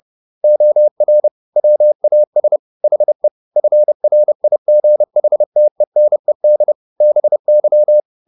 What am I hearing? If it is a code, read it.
Morse code, heard as a keyed tone, about 30 wpm, 600 Hz. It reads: OR WAS HE FRIGHTENED BY